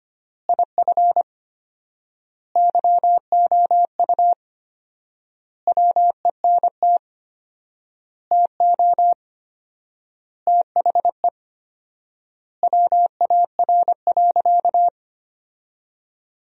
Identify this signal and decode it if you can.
Morse code, heard as a keyed tone, about 25 wpm, 700 Hz. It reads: IF YOU WENT TO THE WAR.
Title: 4x Word-Spacing - Sentences from Top 400 Words - 25wpm